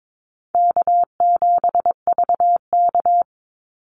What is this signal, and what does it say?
Morse code, heard as a keyed tone, about 22 wpm, 700 Hz. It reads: K7VK
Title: NA SOTA Activator Callsigns 22wpm